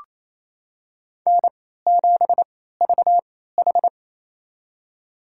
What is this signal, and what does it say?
Morse code, heard as a keyed tone, about 28 wpm, spaced as for 16 wpm, 700 Hz. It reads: N7VH
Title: USA Call Signs - 3x Character Spacing - 28wpm